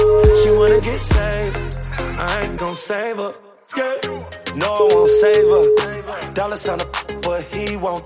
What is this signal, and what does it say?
Ringback tone (Europe, 425 Hz), with noise.